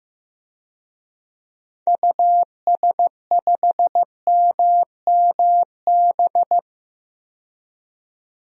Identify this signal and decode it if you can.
Morse code, heard as a keyed tone, about 15 wpm, 700 Hz. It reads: US5MMB
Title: DXCC-100 Call Signs - 6 Characters - 15wpm